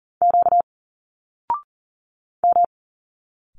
Morse code, heard as a keyed tone, about 39 wpm, 700 Hz. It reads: Q M
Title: Single Letters 40wpm